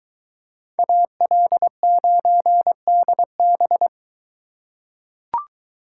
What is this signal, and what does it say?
Morse code, heard as a keyed tone, about 23 wpm, 700 Hz. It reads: AL9DB